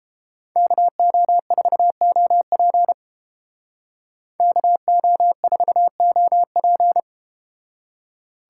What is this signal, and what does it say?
Morse code, heard as a keyed tone, about 31 wpm, 700 Hz. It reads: KO4OP KO4OP